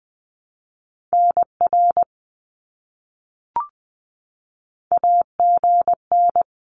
Morse code, heard as a keyed tone, about 20 wpm, 700 Hz. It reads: NR AGN
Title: Abbreviations Repeated 3x - 20wpm